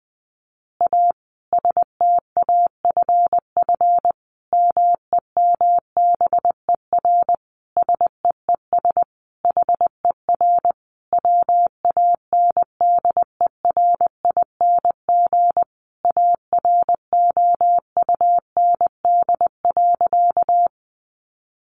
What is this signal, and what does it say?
Morse code, heard as a keyed tone, about 20 wpm, 700 Hz. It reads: A STAFF MEMBER SEES HER WANDERING AROUND.
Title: Additional Easy Phrases 20wpm